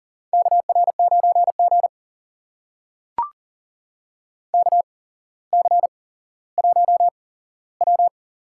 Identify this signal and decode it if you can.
Morse code, heard as a keyed tone, about 42 wpm, spaced as for 17 wpm, 690 Hz. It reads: KR9G KC1W
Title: USA Call Signs - 4 Characters - 8x Character Spacing - 40wpm